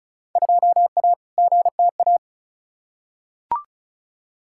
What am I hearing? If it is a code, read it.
Morse code, heard as a keyed tone, about 36 wpm, 700 Hz. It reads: 2A GTA
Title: Field Day Exchanges Encore - 35wpm